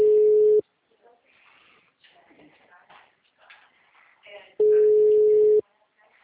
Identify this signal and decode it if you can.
Ringback tone (Europe, 425 Hz), with speech.